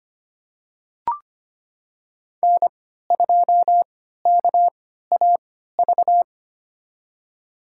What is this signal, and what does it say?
Morse code, heard as a keyed tone, about 25 wpm, spaced as for 14 wpm, 700 Hz. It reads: N2KAV
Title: USA Call Signs - 3x Character Spacing - 25wpm